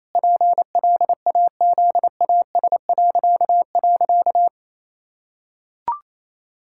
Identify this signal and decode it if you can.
Morse code, heard as a keyed tone, about 28 wpm, 700 Hz. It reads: PLAZAS..
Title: Pangram Challenge v2 - 28wpm